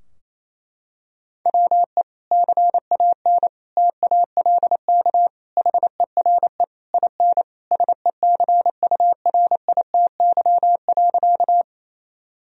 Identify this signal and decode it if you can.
Morse code, heard as a keyed tone, about 28 wpm, 700 Hz. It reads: WE CAN TALK HERE IN SECURITY.
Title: Sentences from Top 1000 Words - Review Part 4 - 28wpm